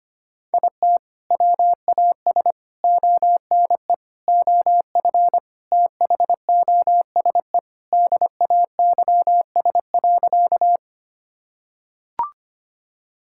Morse code, heard as a keyed tone, about 25 wpm, 700 Hz. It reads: IT WAS ONE OF THOSE DAYS.
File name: practice / 25wpm / Sentences with 3-Word Phrases - 25wpm